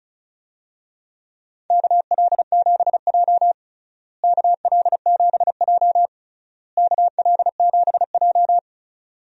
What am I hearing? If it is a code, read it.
Morse code, heard as a keyed tone, about 34 wpm, 700 Hz. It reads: KL7J KL7J KL7J
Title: DXCC-100 Call Signs - 4 Characters - Repeated 3x - 35wpm